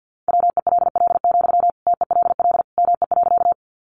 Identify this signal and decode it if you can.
Morse code, heard as a keyed tone, about 50 wpm, 700 Hz. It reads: WELL, TELL ME.
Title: Sentences from Top 200 Words 50wpm